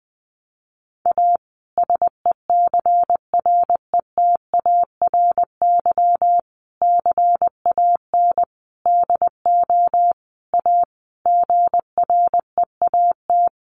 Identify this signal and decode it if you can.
Morse code, heard as a keyed tone, about 20 wpm, 700 Hz. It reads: A SECRETARY CAN DO A GREAT
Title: Sentences from Top 1300 Words 20wpm